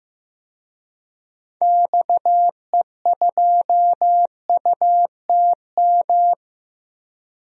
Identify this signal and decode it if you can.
Morse code, heard as a keyed tone, about 15 wpm, 700 Hz. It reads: XE2UTM
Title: DXCC-100 Call Signs - 6 Characters - Repeated 2x - 15wpm